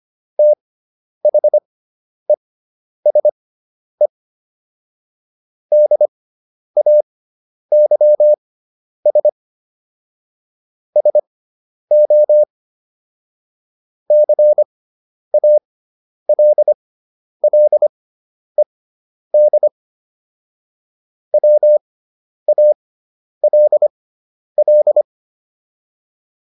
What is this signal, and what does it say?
Morse code, heard as a keyed tone, about 25 wpm, spaced as for 10 wpm, 600 Hz. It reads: THESE DAYS SO CALLED WALL